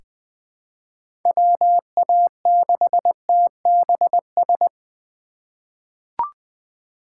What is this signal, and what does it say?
Morse code, heard as a keyed tone, about 20 wpm, 700 Hz. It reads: WA6TBS
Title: USA Call Signs - 4x Character Spacing - 20wpm